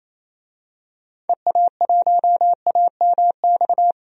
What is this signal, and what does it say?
Morse code, heard as a keyed tone, about 28 wpm, 700 Hz. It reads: EA1AMX
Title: DXCC-100 Call Signs - 6 Characters - 28wpm